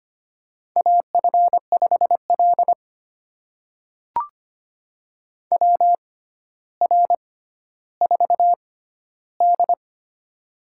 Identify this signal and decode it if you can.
Morse code, heard as a keyed tone, about 25 wpm, spaced as for 13 wpm, 700 Hz. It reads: AF5L WR4D
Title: USA Call Signs - 6x Character Spacing - 25wpm